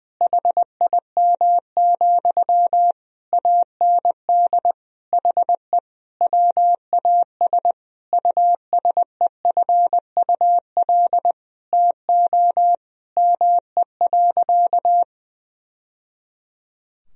Morse code, heard as a keyed tone, about 21 wpm, 700 Hz. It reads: HIM, AND HE WAS USEFUL TO ME.
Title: Sentences from Top 1000 Words - Review Part 4 - 20wpm